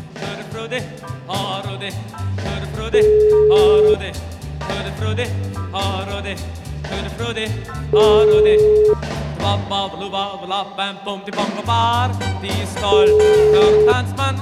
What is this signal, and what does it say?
Ringback tone (Europe, 425 Hz), with speech.